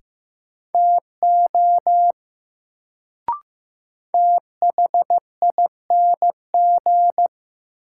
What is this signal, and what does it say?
Morse code, heard as a keyed tone, about 15 wpm, 700 Hz. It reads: TO THING
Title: Sentences from Top 100 Words - New Word Drill - 15wpm